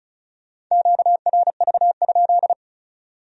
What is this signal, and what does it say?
Morse code, heard as a keyed tone, about 43 wpm, 700 Hz. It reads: QRV?
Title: Speed-Racing Q-Codes - 35wpm